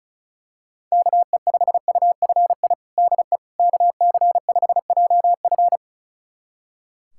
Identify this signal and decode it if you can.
Morse code, heard as a keyed tone, about 36 wpm, 700 Hz. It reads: KE5UFI DE KC5JF
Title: USA Callsign DE Callsign 35wpm